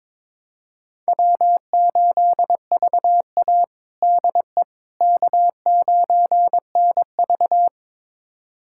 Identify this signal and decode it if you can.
Morse code, heard as a keyed tone, about 22 wpm, 700 Hz. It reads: W8VA DE K9NV
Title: USA Callsign DE Callsign - 4 Characters - 22wpm